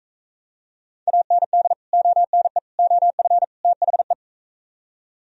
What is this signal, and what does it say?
Morse code, heard as a keyed tone, about 41 wpm, 700 Hz. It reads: AND ONE OF THE